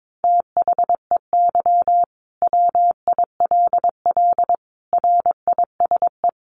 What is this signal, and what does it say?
Morse code, heard as a keyed tone, about 22 wpm, 700 Hz. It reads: THEY WILL RISE
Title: Sentences from Top 900 Words 22wpm